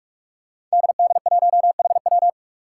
Morse code, heard as a keyed tone, about 46 wpm, 700 Hz. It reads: DD1HW
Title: Five Challenging Letter-Number Combinations 45wpm